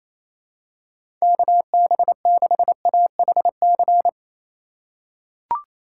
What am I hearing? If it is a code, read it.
Morse code, heard as a keyed tone, about 28 wpm, 700 Hz. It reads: KB6AHC
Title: USA Call Signs - 6 Characters - 2x Character Spacing - 28wpm